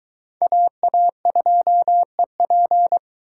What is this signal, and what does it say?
Morse code, heard as a keyed tone, about 21 wpm, 700 Hz. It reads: AA2EP